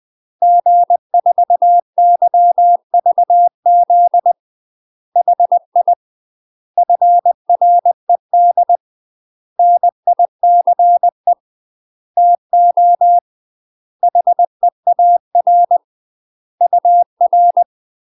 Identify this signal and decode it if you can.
Morse code, heard as a keyed tone, about 20 wpm, 700 Hz. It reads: G4YVZ HI FRED NICE TO HEAR UR
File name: UK MASTER QSOs TRAINING / 20WPM / UK MASTER QSOs 0037